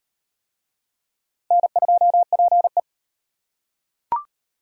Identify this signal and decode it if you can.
Morse code, heard as a keyed tone, about 37 wpm, 700 Hz. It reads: N2PE